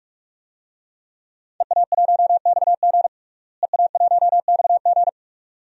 Morse code, heard as a keyed tone, about 41 wpm, 700 Hz. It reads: EA1XG EA1XG
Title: DXCC-100 Call Signs - 5 Characters - Repeated 2x - 45wpm